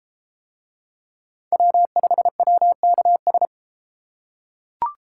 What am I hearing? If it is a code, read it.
Morse code, heard as a keyed tone, about 33 wpm, 700 Hz. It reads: W5WKS